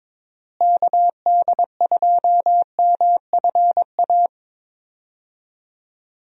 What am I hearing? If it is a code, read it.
Morse code, heard as a keyed tone, about 22 wpm, 700 Hz. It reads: KD2MFA
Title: USA Call Signs - 6 Characters - Repeated 3x - 22wpm